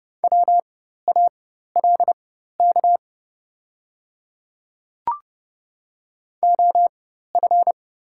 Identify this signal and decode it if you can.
Morse code, heard as a keyed tone, about 29 wpm, spaced as for 14 wpm, 700 Hz. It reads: WALK OF
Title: 4x Character Spacing - Top 500 Words - 30wpm